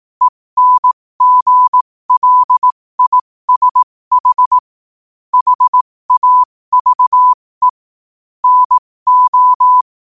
Morse code, heard as a keyed tone, about 18 wpm, spaced as for 15 wpm, 1000 Hz. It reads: ENGLISH HAVE NO
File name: qotdfiles / 15wpm